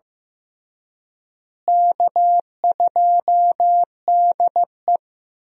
Morse code, heard as a keyed tone, about 15 wpm, 700 Hz. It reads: K2DE
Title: USA Call Signs - 4 Characters - Repeated 2x - 15wpm